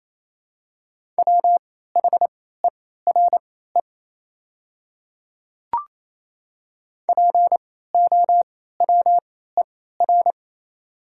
Morse code, heard as a keyed tone, about 28 wpm, spaced as for 16 wpm, 700 Hz. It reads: WHERE POWER
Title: 3x Character Spacing - Top 500 Words - 28wpm